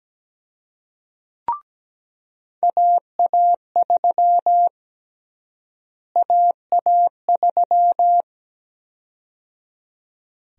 Morse code, heard as a keyed tone, about 17 wpm, 700 Hz. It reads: AA3 AA3
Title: USA Call Sign Prefixes - Repeated 2x - 17wpm